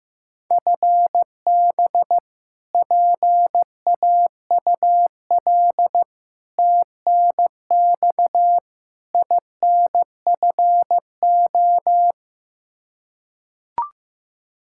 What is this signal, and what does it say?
Morse code, heard as a keyed tone, about 15 wpm, 700 Hz. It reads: FB PAUL TNX INFO
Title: Practice QSO - Exchange 3 - More Info - 15wpm